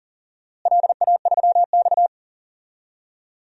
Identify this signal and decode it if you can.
Morse code, heard as a keyed tone, about 40 wpm, 700 Hz. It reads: LA3X
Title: DXCC-100 Call Signs - 4 Characters - Repeated 2x - 40wpm